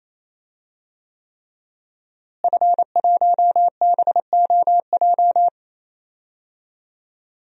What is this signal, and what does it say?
Morse code, heard as a keyed tone, about 28 wpm, 700 Hz. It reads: F1BOJ
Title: DXCC-100 Call Signs - 5 Characters - Repeated 3x - 28wpm